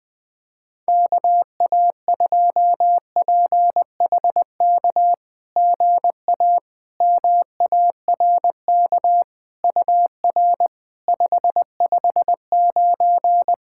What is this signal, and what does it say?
Morse code, heard as a keyed tone, about 20 wpm, 690 Hz. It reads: KA2PHK GA MARK UR 559
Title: SOTA QSO Practice 20wpm